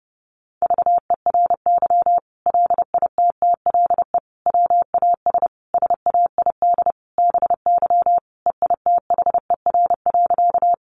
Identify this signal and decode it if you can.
Morse code, heard as a keyed tone, about 28 wpm, 700 Hz. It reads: VERY LITTLE WAS SAID BY EITHER.
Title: Sentences from Top 600 Words 30wpm